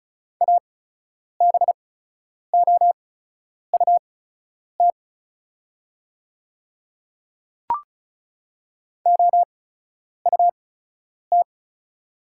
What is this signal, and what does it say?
Morse code, heard as a keyed tone, about 36 wpm, spaced as for 10 wpm, 700 Hz. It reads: ABOUT OUT
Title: 8x Character Spacing - Top 100 Words - 35wpm